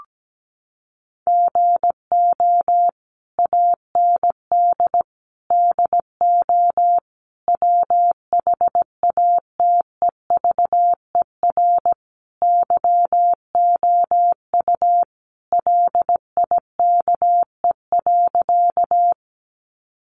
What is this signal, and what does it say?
Morse code, heard as a keyed tone, about 17 wpm, 700 Hz. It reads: GO AND DO WHATEVER YOU LIKE.
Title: Sentences from Top 1100 Words 17wpm